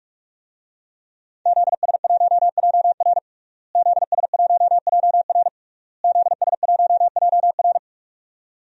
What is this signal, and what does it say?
Morse code, heard as a keyed tone, about 41 wpm, 700 Hz. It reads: ZS1JR ZS1JR ZS1JR
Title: DXCC-100 Call Signs - 5 Characters - Repeated 3x - 45wpm